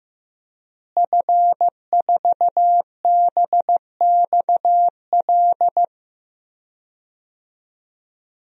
Morse code, heard as a keyed tone, about 15 wpm, 700 Hz. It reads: F4BXL
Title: DXCC-100 Call Signs - 5 Characters - Repeated 3x - 15wpm